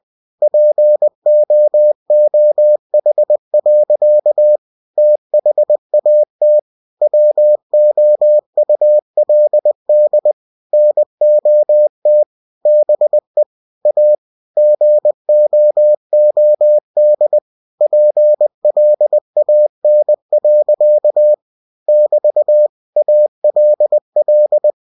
Morse code, heard as a keyed tone, about 20 wpm, 600 Hz. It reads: POOH. THAT WOULD NOT BE A GOOD PLAN. <BT> ALL
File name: cw-books / winnie-the-pooh / winnie-20wpm-page-0049